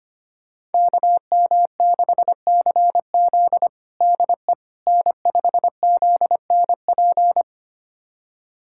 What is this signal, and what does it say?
Morse code, heard as a keyed tone, about 25 wpm, 700 Hz. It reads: KM6CZ DE N5ZNP
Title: USA Callsign DE Callsign - 5 Characters - 25wpm